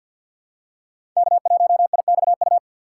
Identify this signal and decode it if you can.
Morse code, heard as a keyed tone, about 50 wpm, 700 Hz. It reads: K1IXU
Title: USA Call Signs - 5 Characters - 4x Character Spacing - 50wpm